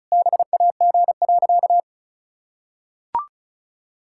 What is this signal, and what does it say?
Morse code, heard as a keyed tone, about 36 wpm, 700 Hz. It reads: BAG.
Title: Pangram Challenge v2 - 35wpm